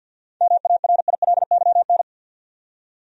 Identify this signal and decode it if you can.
Morse code, heard as a keyed tone, about 51 wpm, 700 Hz. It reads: MARILYN
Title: USA Common Names - 50wpm